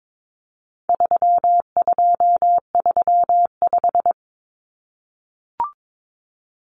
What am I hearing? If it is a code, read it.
Morse code, heard as a keyed tone, about 22 wpm, 700 Hz. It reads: 3235
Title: Numbers - 4-Digit 22wpm